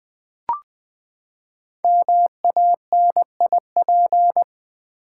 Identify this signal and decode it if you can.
Morse code, heard as a keyed tone, about 20 wpm, 700 Hz. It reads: MANIP
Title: Build A Word - 5 Letters At A Time - 10-Letter Words - 20wpm